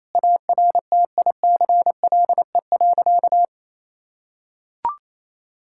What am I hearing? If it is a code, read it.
Morse code, heard as a keyed tone, about 28 wpm, 700 Hz. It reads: ARTICLE.
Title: Sentences from Top 1000 Words - Review Part 5 - 28wpm